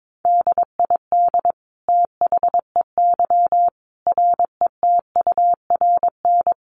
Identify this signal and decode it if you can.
Morse code, heard as a keyed tone, about 22 wpm, 700 Hz. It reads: DID THEY RETURN
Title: Sentences from Top 800 Words 22wpm